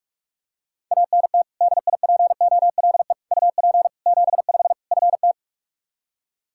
Morse code, heard as a keyed tone, about 46 wpm, 700 Hz. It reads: ANT DIPOLE UP 75 FT
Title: K1USN SST QSO Practice 45wpm